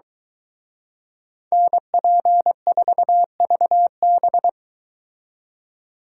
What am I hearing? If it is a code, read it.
Morse code, heard as a keyed tone, about 23 wpm, 700 Hz. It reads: NP4VB